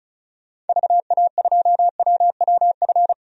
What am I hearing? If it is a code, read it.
Morse code, heard as a keyed tone, about 36 wpm, 700 Hz. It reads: VA2WWF
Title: DXCC-100 Call Signs - 6 Characters - 35wpm